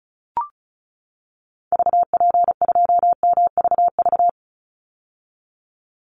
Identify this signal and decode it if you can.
Morse code, heard as a keyed tone, about 34 wpm, 700 Hz. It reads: VP2MVV
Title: Call Signs - 35wpm